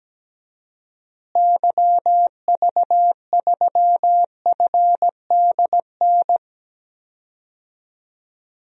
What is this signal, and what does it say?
Morse code, heard as a keyed tone, about 17 wpm, 700 Hz. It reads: YV3FDN
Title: DXCC-100 Call Signs - 6 Characters - Repeated 3x - 17wpm